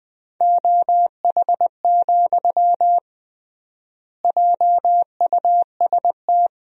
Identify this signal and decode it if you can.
Morse code, heard as a keyed tone, about 20 wpm, 700 Hz. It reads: OH, JUST
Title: 3x Word-Spacing - Sentences from Top 500 Words - 20wpm